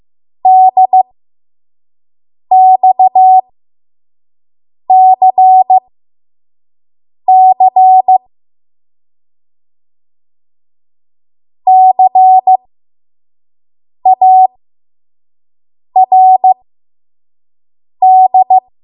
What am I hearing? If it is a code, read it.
Morse code, heard as a keyed tone, about 15 wpm, spaced as for 5 wpm, 750 Hz. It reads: DXCC CARD
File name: Morse / 05WPM